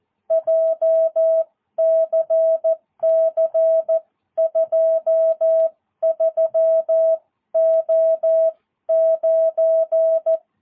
Morse code, heard as a keyed tone, about 14 wpm, spaced as for 12 wpm, 660 Hz. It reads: JCC 23O9
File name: jh8cbh / 2022cw